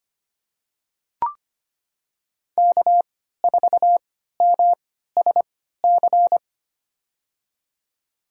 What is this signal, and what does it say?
Morse code, heard as a keyed tone, about 25 wpm, spaced as for 14 wpm, 700 Hz. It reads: K4MSC
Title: USA Call Signs - 3x Character Spacing - 25wpm